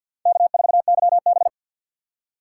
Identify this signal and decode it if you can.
Morse code, heard as a keyed tone, about 52 wpm, 690 Hz. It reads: K4YB
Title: USA Call Signs - 4 Characters - 6x Character Spacing - 50wpm